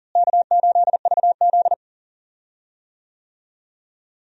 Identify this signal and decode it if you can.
Morse code, heard as a keyed tone, about 40 wpm, 700 Hz. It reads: K8VZ
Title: USA Call Signs - 4 Characters - Repeated 3x - 40wpm